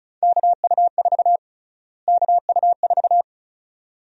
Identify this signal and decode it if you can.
Morse code, heard as a keyed tone, about 34 wpm, 700 Hz. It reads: KU4 KU4
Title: USA Call Sign Prefixes - Repeated 2x - 35wpm